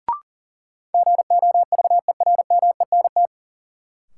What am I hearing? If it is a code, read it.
Morse code, heard as a keyed tone, about 40 wpm, 700 Hz. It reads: GOVERMENT
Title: Sentences from Top 1300 Words - New Word Drill Alt - 40wpm